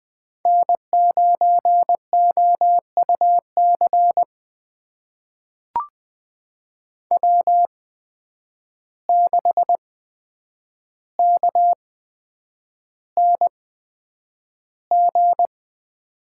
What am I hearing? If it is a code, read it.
Morse code, heard as a keyed tone, about 20 wpm, spaced as for 9 wpm, 700 Hz. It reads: N9OUC W6KNG
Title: USA Call Signs - 5 Characters - 8x Character Spacing - 20wpm